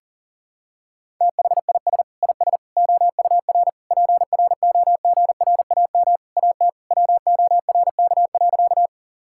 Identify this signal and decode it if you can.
Morse code, heard as a keyed tone, about 37 wpm, 700 Hz. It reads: THIS IS OUR PROGRAM AT WORK.
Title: Sentences from Top 1000 Words - Review Part 5 - 40wpm